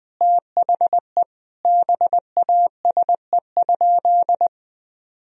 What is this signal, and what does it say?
Morse code, heard as a keyed tone, about 20 wpm, 700 Hz. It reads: THE BASE?
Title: Sentences from Top 1000 Words - Review Part 1 - 20wpm